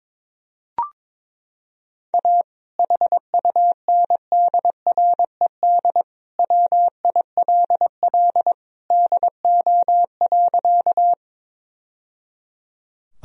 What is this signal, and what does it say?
Morse code, heard as a keyed tone, about 22 wpm, 700 Hz. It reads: A HUNDRED WILL DO.
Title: Sentences from Top 1000 Words - Review Part 2 - 22wpm